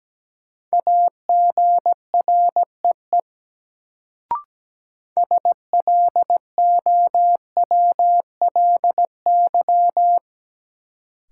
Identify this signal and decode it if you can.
Morse code, heard as a keyed tone, about 17 wpm, 700 Hz. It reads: AGREE SLOWLY
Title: Sentences from Top 1100 Words - New Word Drill Alt - 17wpm